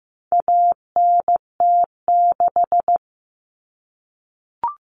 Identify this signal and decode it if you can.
Morse code, heard as a keyed tone, about 15 wpm, 700 Hz. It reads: ANT6
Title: Common Cut Numbers - 4-Digit 15wpm